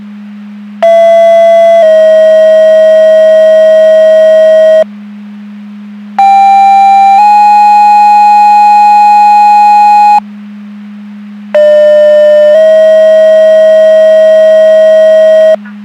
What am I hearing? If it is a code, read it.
Two-tone page, heard as a one-second tone, then a three-second tone, about 670 then 640 Hz, about 790 then 830 Hz, about 600 then 640 Hz, with noise. It is heard from 0.8 to 15.5 s.
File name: Portage County / 00005 15-45 Portage County 154.3850 Fire EMS Paging